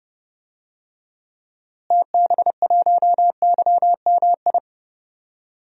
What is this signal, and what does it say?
Morse code, heard as a keyed tone, about 30 wpm, 700 Hz. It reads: TB1YMI
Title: DXCC-100 Call Signs - 6 Characters - Repeated 2x - 30wpm